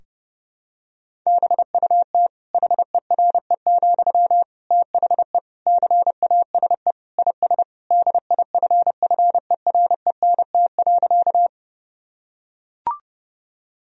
Morse code, heard as a keyed tone, about 30 wpm, 700 Hz. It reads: BUT HERE, THE CASE IS DIFFERENT.
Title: Sentences from Top 1000 Words - Review Part 3 - 30wpm